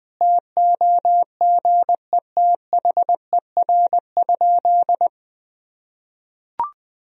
Morse code, heard as a keyed tone, about 21 wpm, 700 Hz. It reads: TOGETHER?
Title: Sentences from Top 1000 Words - Review Part 1 - 20wpm